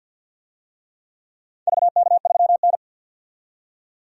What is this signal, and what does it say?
Morse code, heard as a keyed tone, about 48 wpm, 700 Hz. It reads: VK3N
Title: DXCC-100 Call Signs - 4 Characters - 50wpm